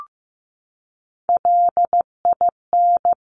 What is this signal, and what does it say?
Morse code, heard as a keyed tone, about 15 wpm, 700 Hz. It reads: LIN
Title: Top Three Letter Combinations in Words 15wpm